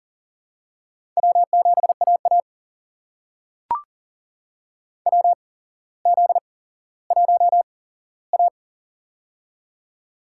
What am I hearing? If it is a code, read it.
Morse code, heard as a keyed tone, about 42 wpm, spaced as for 17 wpm, 700 Hz. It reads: W7AA WZ1A
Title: USA Call Signs - 4 Characters - 8x Character Spacing - 40wpm